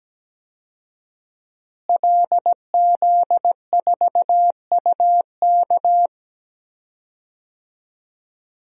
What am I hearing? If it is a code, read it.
Morse code, heard as a keyed tone, about 17 wpm, 700 Hz. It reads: LZ4UK
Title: DXCC-100 Call Signs - 5 Characters - Repeated 3x - 17wpm